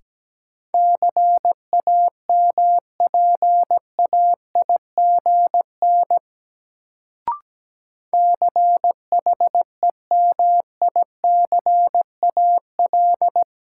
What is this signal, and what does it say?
Morse code, heard as a keyed tone, about 17 wpm, 700 Hz. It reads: CAMPAIGN CHEMICAL
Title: Sentences from Top 1200 Words - New Word Drill - 17wpm